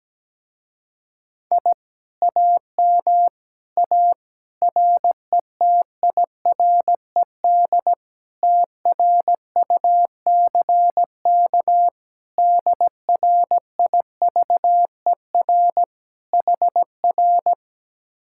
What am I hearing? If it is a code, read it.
Morse code, heard as a keyed tone, about 17 wpm, 700 Hz. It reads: I AM A RETIRED TRUCK DRIVER HR
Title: Speed-Racing Common QSO Phrases - 17wpm